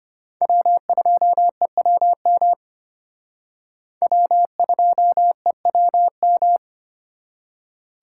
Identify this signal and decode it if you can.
Morse code, heard as a keyed tone, about 31 wpm, 700 Hz. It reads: W2EWM W2EWM
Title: Speed Racing - USA 1x3 Call Signs 25wpm